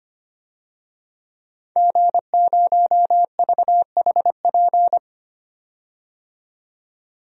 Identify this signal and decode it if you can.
Morse code, heard as a keyed tone, about 25 wpm, 700 Hz. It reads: G0VHP
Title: DXCC-100 Call Signs - 5 Characters - 25wpm